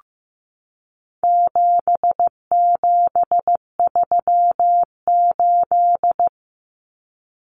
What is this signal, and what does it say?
Morse code, heard as a keyed tone, about 15 wpm, 700 Hz. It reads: 7738
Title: Numbers - 4-Digit 15wpm